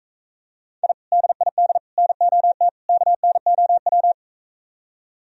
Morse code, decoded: I DID NOT KNOW

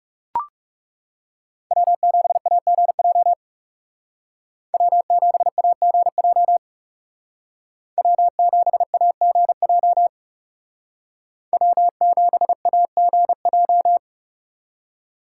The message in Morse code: W7AGJ W7AGJ W7AGJ W7AGJ

